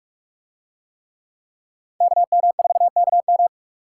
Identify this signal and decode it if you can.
Morse code, heard as a keyed tone, about 42 wpm, 700 Hz. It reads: KM4KM